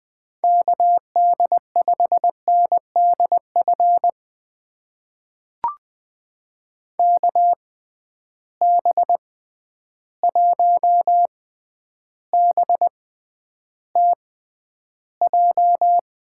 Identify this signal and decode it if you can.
Morse code, heard as a keyed tone, about 20 wpm, spaced as for 10 wpm, 700 Hz. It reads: KD5NDF KB1BTJ
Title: USA Call Signs - 6 Characters - 6x Character Spacing - 20wpm